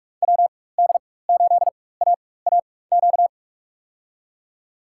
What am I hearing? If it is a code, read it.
Morse code, heard as a keyed tone, about 44 wpm, spaced as for 21 wpm, 700 Hz. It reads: WD8AAQ
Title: USA Call Signs - 6 Characters - 4x Character Spacing - 45wpm